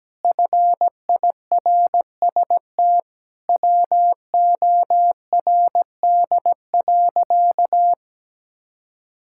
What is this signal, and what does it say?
Morse code, heard as a keyed tone, about 17 wpm, 700 Hz. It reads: FIRST WORD.
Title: Pangram Challenge v2 - 17wpm